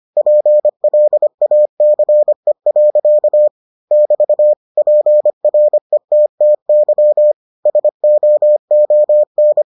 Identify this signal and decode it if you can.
Morse code, heard as a keyed tone, about 25 wpm, 600 Hz. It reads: PLACE. <BT> PRETTY SOON